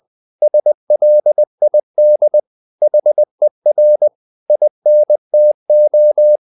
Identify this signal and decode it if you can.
Morse code, heard as a keyed tone, about 20 wpm, 600 Hz. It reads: SLID HER INTO
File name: the old man and the sea / the old man and the sea part 1 20wpm